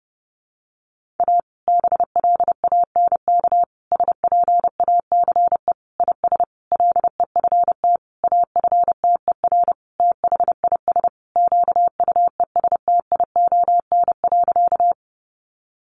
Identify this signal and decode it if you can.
Morse code, heard as a keyed tone, about 30 wpm, 700 Hz. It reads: A BLANK SPACE IS LEFT AFTER THIS QUESTION.
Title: Sentences from Top 1500 Words 30wpm